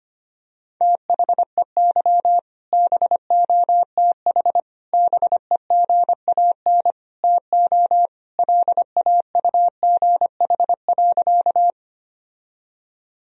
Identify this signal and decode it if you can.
Morse code, heard as a keyed tone, about 25 wpm, 700 Hz. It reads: THEY BOTH BEGAN TO LAUGH.
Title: Sentences from Top 1000 Words - Review Part 1 - 25wpm